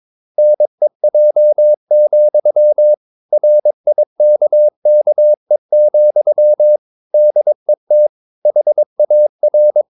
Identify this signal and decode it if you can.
Morse code, heard as a keyed tone, about 22 wpm, 600 Hz. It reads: NEJ, RIKKE, DET HAR
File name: Ferie QRS 22wpm